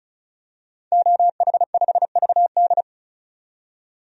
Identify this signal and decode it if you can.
Morse code, heard as a keyed tone, about 34 wpm, 700 Hz. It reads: OH5VD
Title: DXCC-100 Call Signs - 5 Characters - 35wpm